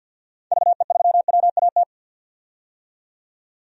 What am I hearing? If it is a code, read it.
Morse code, heard as a keyed tone, about 53 wpm, 700 Hz. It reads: VE3WAT
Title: DXCC-100 Call Signs - 6 Characters - 50wpm